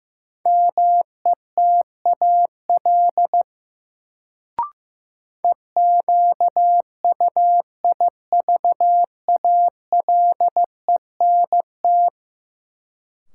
Morse code, heard as a keyed tone, about 15 wpm, 700 Hz. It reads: METAL EQUIVALENT
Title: Sentences from Top 1600 Words - New Word Drill Alt - 15wpm